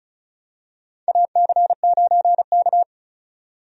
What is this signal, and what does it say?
Morse code, heard as a keyed tone, about 36 wpm, 700 Hz. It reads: AC9K
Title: USA Call Signs - 4 Characters - 35wpm